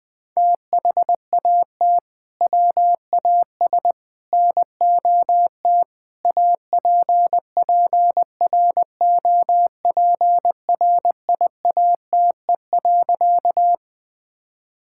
Morse code, heard as a keyed tone, about 21 wpm, 700 Hz. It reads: THAT WAS NOT APPROPRIATE.
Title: Sentences from Top 1000 Words - Review Part 5 - 20wpm